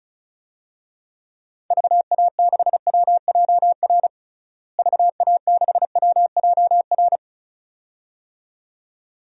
Morse code, heard as a keyed tone, about 34 wpm, 700 Hz. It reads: VA6WJR VA6WJR
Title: DXCC-100 Call Signs - 6 Characters - Repeated 2x - 35wpm